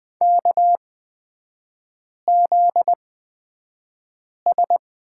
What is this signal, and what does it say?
Morse code, heard as a keyed tone, about 20 wpm, spaced as for 5 wpm, 700 Hz. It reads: KZS
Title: 250 Lesson 40e - BK - Three Characters 20wpm